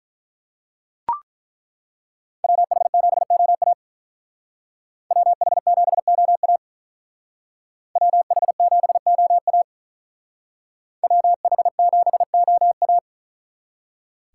Morse code, heard as a keyed tone, about 42 wpm, 700 Hz. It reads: WH7OA WH7OA WH7OA WH7OA
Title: Speed Racing - USA 2x2 Call Signs 35wpm